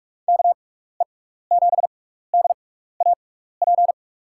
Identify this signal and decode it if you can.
Morse code, heard as a keyed tone, about 44 wpm, spaced as for 15 wpm, 700 Hz. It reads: KE7DAP
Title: USA Call Signs - 6x Character Spacing - 45wpm